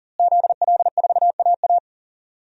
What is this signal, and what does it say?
Morse code, heard as a keyed tone, about 40 wpm, 700 Hz. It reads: ZL4AA
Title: DXCC-100 Call Signs - 5 Characters - Repeated 3x - 40wpm